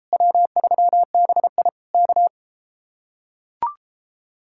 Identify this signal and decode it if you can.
Morse code, heard as a keyed tone, about 33 wpm, 700 Hz. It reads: W3BI K